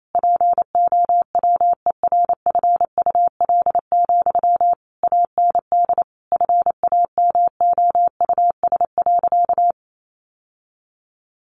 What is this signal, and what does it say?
Morse code, heard as a keyed tone, about 28 wpm, 700 Hz. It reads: POWERFUL, AND FAMOUS.
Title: Sentences from Top 1500 Words 28wpm